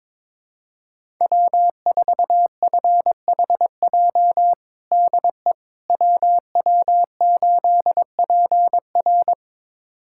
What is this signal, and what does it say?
Morse code, heard as a keyed tone, about 21 wpm, 700 Hz. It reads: W4FHJ DE WW8PR
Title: USA Callsign DE Callsign - 5 Characters - 22wpm